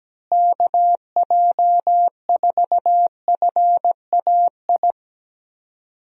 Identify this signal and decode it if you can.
Morse code, heard as a keyed tone, about 17 wpm, 700 Hz. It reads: KJ4FAI
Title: USA Call Signs - 6 Characters - Repeated 3x - 17wpm